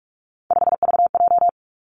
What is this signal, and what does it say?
Morse code, heard as a keyed tone, about 46 wpm, 700 Hz. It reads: 5VJ
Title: Three Challenging Letter-Number Combinations 45wpm